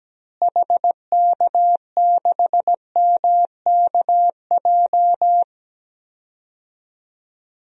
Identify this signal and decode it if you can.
Morse code, heard as a keyed tone, about 17 wpm, 700 Hz. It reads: HK6MKJ
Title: DXCC-100 Call Signs - 6 Characters - Repeated 3x - 17wpm